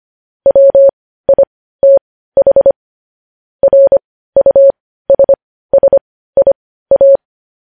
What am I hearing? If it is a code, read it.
Morse code, heard as a keyed tone, about 25 wpm, spaced as for 15 wpm, 550 Hz. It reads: WITH RUSSIA
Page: News Headlines in Morse Code at 15 WPM